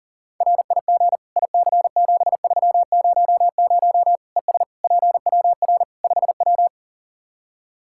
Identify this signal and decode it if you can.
Morse code, heard as a keyed tone, about 40 wpm, 700 Hz. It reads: RIG IC7300 ES PWR 5W
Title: Practice QSO - Exchange 2 - Info - 40wpm